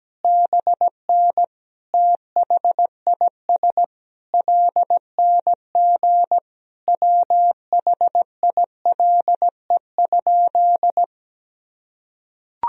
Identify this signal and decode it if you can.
Morse code, heard as a keyed tone, about 17 wpm, 700 Hz. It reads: BN THIS LNG WHILE?
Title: Sentences from Top 300 Words and Abbreviations 17wpm